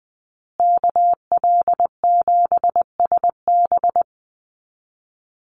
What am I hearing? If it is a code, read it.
Morse code, heard as a keyed tone, about 20 wpm, 700 Hz. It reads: KL7SB
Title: Call Signs Repeated 2x - 20wpm